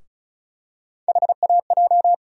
Morse code, heard as a keyed tone, about 36 wpm, 700 Hz. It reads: HAJ
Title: USA Call Sign Postfixes - 3 Letters 35wpm